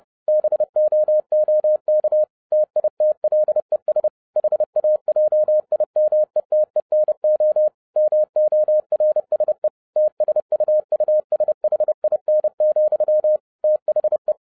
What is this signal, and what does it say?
Morse code, heard as a keyed tone, about 30 wpm, 610 Hz. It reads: BOOK TITLES HAJIMETENO MORSE TSUUSHIN, THE